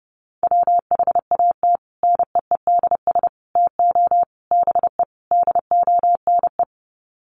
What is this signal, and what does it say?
Morse code, heard as a keyed tone, about 28 wpm, 700 Hz. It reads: WHAT NEEDS TO BE DONE
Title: Sets of 5 Words Encore 30wpm